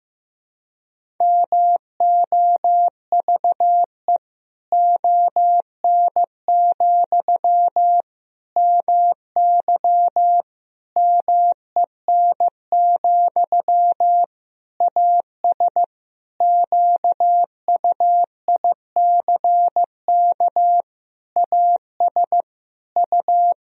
Morse code, heard as a keyed tone, about 15 wpm, 700 Hz. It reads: MOVE ON, MY MEN, AS QUICK AS U
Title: Sentences from Top 500 Words and Abbreviations 15wpm